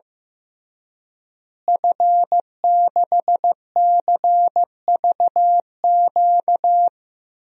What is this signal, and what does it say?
Morse code, heard as a keyed tone, about 15 wpm, 700 Hz. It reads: F6CVQ
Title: DXCC-100 Call Signs - 5 Characters - Repeated 3x - 15wpm